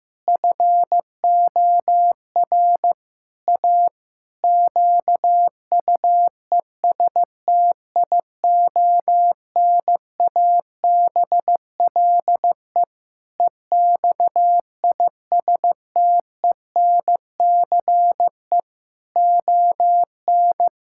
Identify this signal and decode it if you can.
Morse code, heard as a keyed tone, about 16 wpm, 700 Hz. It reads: FOR A QUESTIONABLE EXISTENCE ON
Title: Pangram Challenge v2 - 15wpm